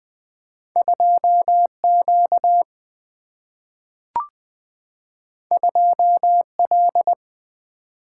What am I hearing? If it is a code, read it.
Morse code, heard as a keyed tone, about 20 wpm, 700 Hz. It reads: 2 Q 2 L
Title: Single-Letter Single-Number 20wpm